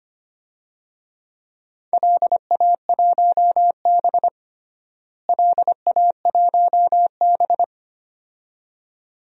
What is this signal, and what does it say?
Morse code, heard as a keyed tone, about 25 wpm, 700 Hz. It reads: LA1B LA1B
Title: DXCC-100 Call Signs - 4 Characters - Repeated 2x - 25wpm